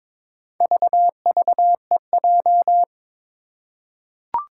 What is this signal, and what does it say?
Morse code, heard as a keyed tone, about 23 wpm, 700 Hz. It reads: VVEJ
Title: Four Challenging Letter-Number Combinations 22wpm